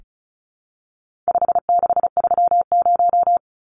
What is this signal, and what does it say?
Morse code, heard as a keyed tone, about 34 wpm, 700 Hz. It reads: 5630
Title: Numbers - 4-Digit 35wpm